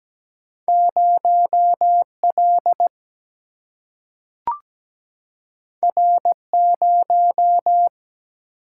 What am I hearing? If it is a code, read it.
Morse code, heard as a keyed tone, about 17 wpm, 700 Hz. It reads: 0 L R 0
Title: Single-Letter Single-Number 17wpm